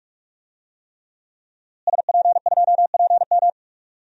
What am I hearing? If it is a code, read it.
Morse code, heard as a keyed tone, about 43 wpm, 700 Hz. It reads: SP2PM
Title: DXCC-100 Call Signs - 5 Characters - Repeated 3x - 45wpm